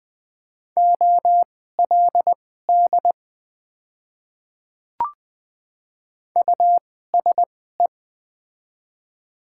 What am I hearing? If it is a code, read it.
Morse code, heard as a keyed tone, about 20 wpm, spaced as for 15 wpm, 700 Hz. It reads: OLD USE
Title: 2x Character Spacing - Top 200 Words - 20wpm